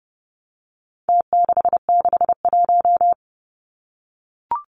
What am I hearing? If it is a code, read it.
Morse code, heard as a keyed tone, about 30 wpm, 700 Hz. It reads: T661
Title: Numbers - Contest Style - 4-Digit 30wpm